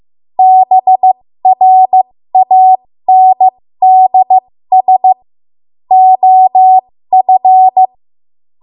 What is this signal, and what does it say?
Morse code, heard as a keyed tone, about 15 wpm, spaced as for 13 wpm, 750 Hz. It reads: BRANDS OF
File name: Morse / 13WPM